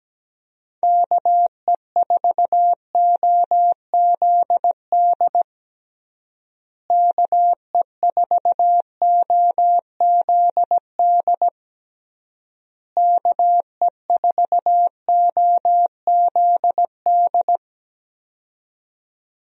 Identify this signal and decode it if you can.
Morse code, heard as a keyed tone, about 17 wpm, 700 Hz. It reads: KE4OZD KE4OZD KE4OZD
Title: USA Call Signs - 6 Characters - Repeated 3x - 17wpm